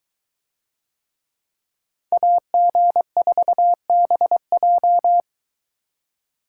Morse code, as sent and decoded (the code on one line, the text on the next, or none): .- --. ....- -... .---
AG4BJ